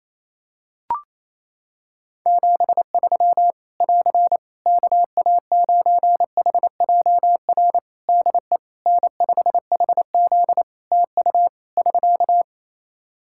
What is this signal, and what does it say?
Morse code, heard as a keyed tone, about 27 wpm, 700 Hz. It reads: 73 <AR> KA9HJR DE N5HZ TU <SK>
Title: Practice QSO - Exchange 4 - Ending - 28wpm